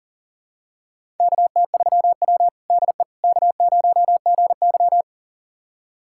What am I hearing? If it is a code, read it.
Morse code, heard as a keyed tone, about 40 wpm, 700 Hz. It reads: KT3W DE K0GY
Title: USA Callsign DE Callsign - 4 Characters - 40wpm